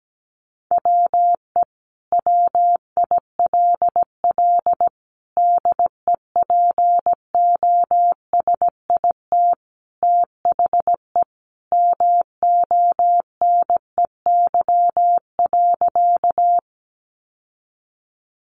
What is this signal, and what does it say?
Morse code, heard as a keyed tone, about 17 wpm, 700 Hz. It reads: WE WILL DEPOSIT THE MONEY.
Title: Sentences from Top 1300 Words 17wpm